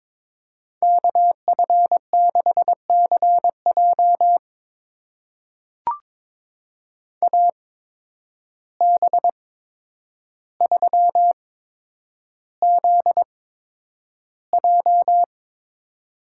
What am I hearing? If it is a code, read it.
Morse code, heard as a keyed tone, about 22 wpm, spaced as for 9 wpm, 700 Hz. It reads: KF6CJ AB3ZJ
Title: USA Call Signs - 5 Characters - 8x Character Spacing - 22wpm